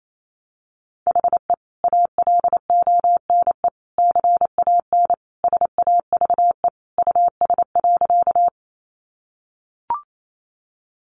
Morse code, heard as a keyed tone, about 28 wpm, 700 Hz. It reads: HE ALONE CAN SAVE US.
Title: Sentences from Top 1100 Words 28wpm